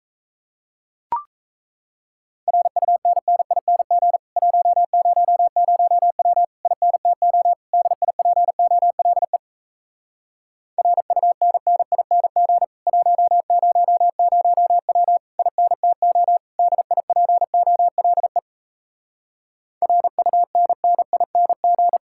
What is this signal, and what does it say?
Morse code, heard as a keyed tone, about 37 wpm, 700 Hz. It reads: RUNNING 100W INTO DIPOLE RUNNING 100W INTO DIPOLE RUNNING